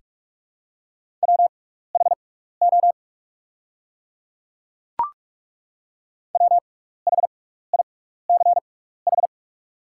Morse code, heard as a keyed tone, about 43 wpm, spaced as for 15 wpm, 690 Hz. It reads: WHO WHICH
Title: 6x Character Spacing - Top 100 Words - 45wpm